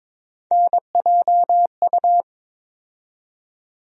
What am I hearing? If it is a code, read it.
Morse code, heard as a keyed tone, about 22 wpm, 700 Hz. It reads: NJU
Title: USA Call Sign Postfixes - 3 Letters 22wpm